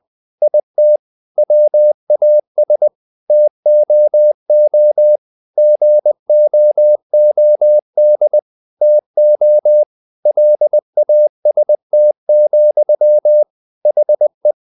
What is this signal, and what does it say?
Morse code, heard as a keyed tone, about 20 wpm, 600 Hz. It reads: IT WAS TOO GOOD TO LAST, HE
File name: the old man and the sea part 2 20wpm